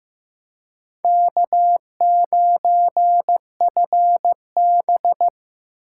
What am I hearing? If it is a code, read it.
Morse code, heard as a keyed tone, about 18 wpm, 700 Hz. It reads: K9FB
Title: Speed Racing - USA 1x2 Call Signs 15wpm